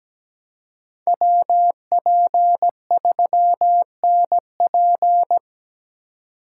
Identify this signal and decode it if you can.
Morse code, heard as a keyed tone, about 20 wpm, 700 Hz. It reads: WP3NP
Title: Speed Racing - USA 2x2 Call Signs 17wpm